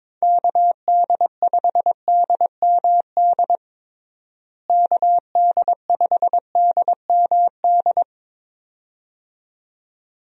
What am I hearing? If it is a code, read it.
Morse code, heard as a keyed tone, about 21 wpm, 700 Hz. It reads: KD5DMD KD5DMD
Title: USA Call Signs - 6 Characters - Repeated 2x - 22wpm